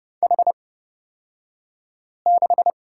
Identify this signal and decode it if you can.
Morse code, heard as a keyed tone, about 30 wpm, spaced as for 5 wpm, 700 Hz. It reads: H6
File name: Lesson 38d - Letter q - Two Characters 30wpm